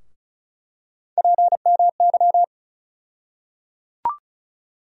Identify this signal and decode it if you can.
Morse code, heard as a keyed tone, about 34 wpm, 700 Hz. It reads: PMY
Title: USA Call Sign Postfixes - 3 Letters 35wpm